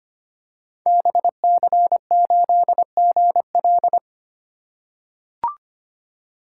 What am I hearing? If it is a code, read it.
Morse code, heard as a keyed tone, about 25 wpm, 700 Hz. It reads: BC8GL
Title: Code Groups - 5 Letter:Numbers - 25wpm